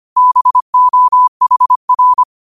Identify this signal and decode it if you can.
Morse code, heard as a keyed tone, about 25 wpm, 1000 Hz. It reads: BOHR
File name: 25wpm